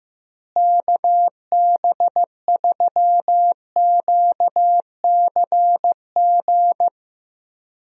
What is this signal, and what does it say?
Morse code, heard as a keyed tone, about 15 wpm, 700 Hz. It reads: KB3QCG
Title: USA Call Signs - 6 Characters - Repeated 3x - 15wpm